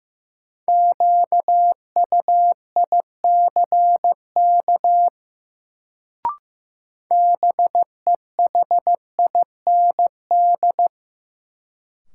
Morse code, heard as a keyed tone, about 15 wpm, 700 Hz. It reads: QUICK BEHIND
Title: Sentences from Top 500 Words - New Word Drill - 15wpm